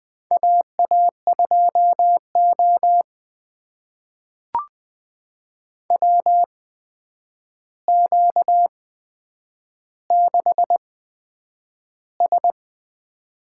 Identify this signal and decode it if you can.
Morse code, heard as a keyed tone, about 20 wpm, spaced as for 9 wpm, 700 Hz. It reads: AA2O WQ6S
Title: USA Call Signs - 4 Characters - 8x Character Spacing - 20wpm